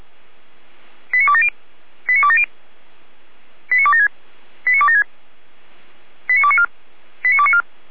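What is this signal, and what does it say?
Five-tone selective call (ZVEI), heard as quick runs of five tones, tones of about 70 ms. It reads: 89270 89270 89276 89276 89294 89294
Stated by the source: ZVEI five-tone call received with an SDR for the Feuerwehr Orlamünde fire brigade.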